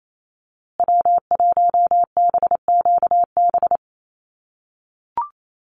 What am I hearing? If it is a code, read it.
Morse code, heard as a keyed tone, about 28 wpm, 700 Hz. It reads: W1BQB
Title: USA Call Signs Repeated 2x 28wpm